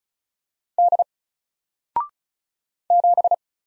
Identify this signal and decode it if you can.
Morse code, heard as a keyed tone, about 35 wpm, 700 Hz. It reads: D 7
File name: Single-Letter-Number-35wpm